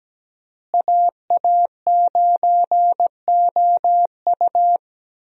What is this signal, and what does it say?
Morse code, heard as a keyed tone, about 17 wpm, 700 Hz. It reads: AA9OU
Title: USA Call Signs - 5 Characters - Repeated 3x - 17wpm